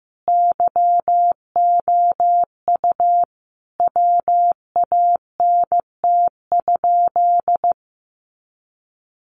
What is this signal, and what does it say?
Morse code, heard as a keyed tone, about 14 wpm, 700 Hz. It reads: YOU WANT?
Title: Sentences from Top 200 Words 15wpm